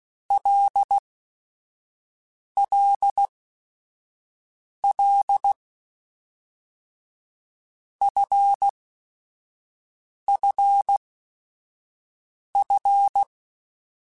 Morse code, heard as a keyed tone, about 16 wpm, spaced as for 5 wpm, 800 Hz. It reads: LLL FFF
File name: Basic Sets / Basic Letter Groups Collective7 / LFCYQ@5